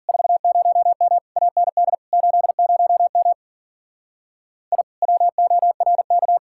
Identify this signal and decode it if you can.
Morse code, heard as a keyed tone, about 48 wpm, 700 Hz. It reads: 40M AND 80M I WORK
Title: Speed-Racing Common QSO Phrases - 40wpm